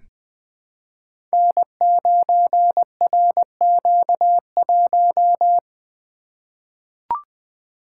Morse code, heard as a keyed tone, about 20 wpm, 700 Hz. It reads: N9RQ1
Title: Code Groups - 5 Letter:Numbers - 20wpm